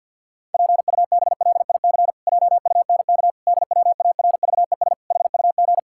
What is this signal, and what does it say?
Morse code, heard as a keyed tone, about 47 wpm, 700 Hz. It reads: PUBLIC JUNK DWARVES HUG
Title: Pangram Challenge v2 - 50wpm